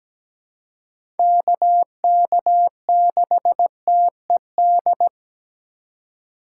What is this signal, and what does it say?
Morse code, heard as a keyed tone, about 17 wpm, 700 Hz. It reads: KK6TED
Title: USA Call Signs - 6 Characters - 4x Character Spacing - 17wpm